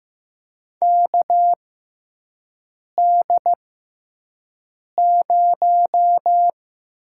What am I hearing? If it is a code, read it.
Morse code, heard as a keyed tone, about 15 wpm, spaced as for 5 wpm, 700 Hz. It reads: KD0
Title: USA Call Signs - 6 Characters - 6x Character Spacing - 15wpm